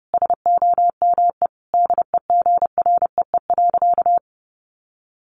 Morse code, heard as a keyed tone, about 30 wpm, 700 Hz. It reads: SOME DEGREE.
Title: Sentences from Top 1500 Words 30wpm